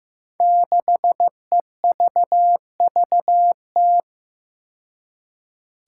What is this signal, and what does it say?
Morse code, heard as a keyed tone, about 15 wpm, 700 Hz. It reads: 6EVVT
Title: Five Challenging Letter-Number Combinations 15wpm